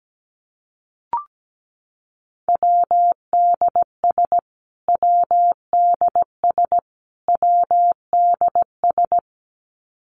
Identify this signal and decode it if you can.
Morse code, heard as a keyed tone, about 17 wpm, 700 Hz. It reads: WDS WDS WDS
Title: Abbreviations Repeated 3x - 17wpm